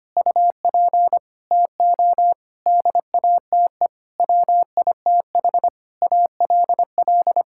Morse code, heard as a keyed tone, about 25 wpm, 700 Hz. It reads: UP TO DATE WITH ALL
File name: Sets of 9 Words - 25wpm